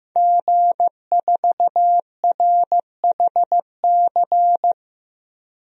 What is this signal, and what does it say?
Morse code, heard as a keyed tone, about 16 wpm, 700 Hz. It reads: G4RHC
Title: DXCC-100 Call Signs - 5 Characters - Repeated 3x - 15wpm